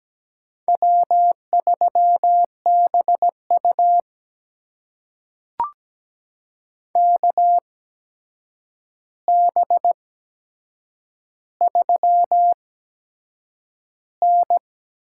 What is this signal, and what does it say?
Morse code, heard as a keyed tone, about 17 wpm, spaced as for 7 wpm, 700 Hz. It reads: W3BU KB3N
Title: USA Call Signs - 4 Characters - 8x Character Spacing - 17wpm